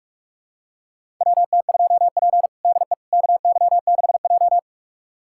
Morse code, heard as a keyed tone, about 46 wpm, 700 Hz. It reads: WT2P DE KY6J
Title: USA Callsign DE Callsign - 4 Characters - 45wpm